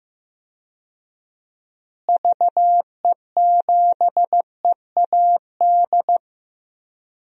Morse code, heard as a keyed tone, about 15 wpm, 700 Hz. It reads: VE7EAD
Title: DXCC-100 Call Signs - 6 Characters - Repeated 3x - 15wpm